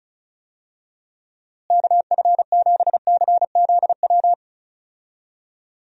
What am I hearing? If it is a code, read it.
Morse code, heard as a keyed tone, about 34 wpm, 700 Hz. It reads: KF7CZW
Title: USA Call Signs - 6 Characters - Repeated 3x - 35wpm